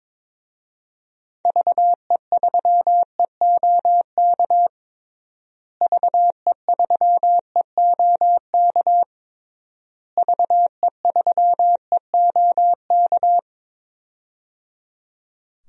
Morse code, heard as a keyed tone, about 22 wpm, 700 Hz. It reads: VE3EOK VE3EOK VE3EOK
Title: DXCC-100 Call Signs - 6 Characters - Repeated 3x - 22wpm